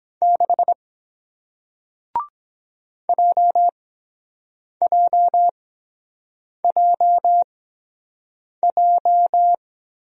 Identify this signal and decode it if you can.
Morse code, heard as a keyed tone, about 23 wpm, spaced as for 7 wpm, 700 Hz. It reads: 6 JJJJ